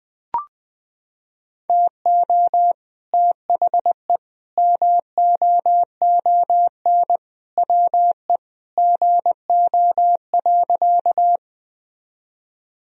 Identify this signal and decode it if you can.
Morse code, heard as a keyed tone, about 20 wpm, 700 Hz. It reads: TO THE MOON WE GO.
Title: Sentences from Top 1000 Words - Review Part 3 - 20wpm